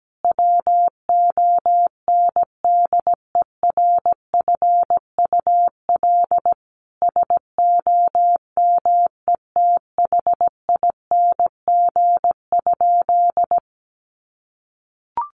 Morse code, heard as a keyed tone, about 17 wpm, 700 Hz. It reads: WONDERFUL SOMETHING?
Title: Sentences from Top 1100 Words 17wpm